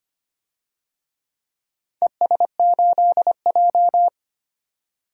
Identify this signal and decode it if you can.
Morse code, heard as a keyed tone, about 25 wpm, 700 Hz. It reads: ES8J
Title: DXCC-100 Call Signs - 4 Characters - 25wpm